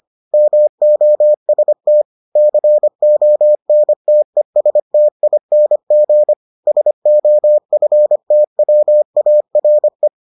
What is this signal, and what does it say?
Morse code, heard as a keyed tone, about 25 wpm, 600 Hz. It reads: MOST CONTESTING SOFTWARE